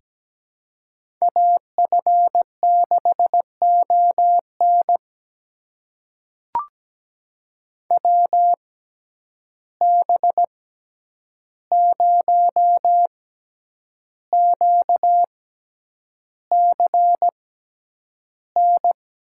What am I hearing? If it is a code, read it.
Morse code, heard as a keyed tone, about 17 wpm, spaced as for 8 wpm, 700 Hz. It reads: AF6ON WB0QCN
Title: USA Call Signs - 6x Character Spacing - 17wpm